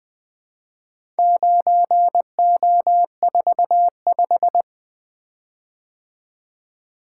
Morse code, heard as a keyed tone, about 20 wpm, 700 Hz. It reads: 9O45
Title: Four Letter-Number Combinations - 20wpm